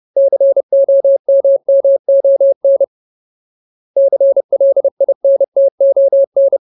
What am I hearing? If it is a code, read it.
Morse code, heard as a keyed tone, about 30 wpm, 550 Hz. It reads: COMMON CLINTON
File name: Ultimate Word List / 30WPM Ultimate